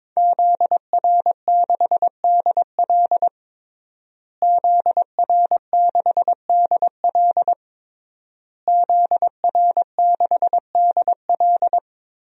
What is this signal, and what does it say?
Morse code, heard as a keyed tone, about 22 wpm, 700 Hz. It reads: ZR6DL ZR6DL ZR6DL
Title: DXCC-100 Call Signs - 5 Characters - Repeated 3x - 22wpm